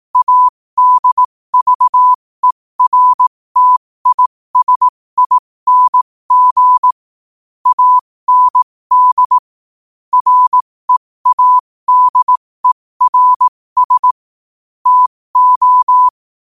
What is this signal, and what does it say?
Morse code, heard as a keyed tone, about 18 wpm, spaced as for 15 wpm, 1000 Hz. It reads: ADVERTISING AND READERS TO
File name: qotdfiles / 15wpm